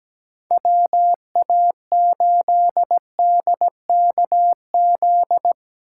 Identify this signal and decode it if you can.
Morse code, heard as a keyed tone, about 17 wpm, 700 Hz. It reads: WA8DKZ
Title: USA Call Signs - 6 Characters - 2x Character Spacing - 17wpm